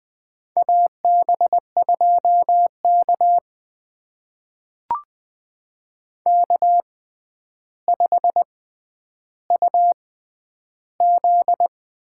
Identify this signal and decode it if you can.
Morse code, heard as a keyed tone, about 20 wpm, spaced as for 10 wpm, 700 Hz. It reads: AB2K K5UZ
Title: USA Call Signs - 4 Characters - 6x Character Spacing - 20wpm